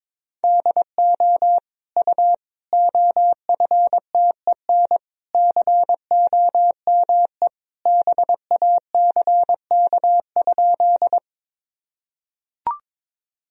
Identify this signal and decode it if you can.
Morse code, heard as a keyed tone, about 22 wpm, 700 Hz. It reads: DO U OFTEN COME BACK?
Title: Sentences from Top 300 Words and Abbreviations 22wpm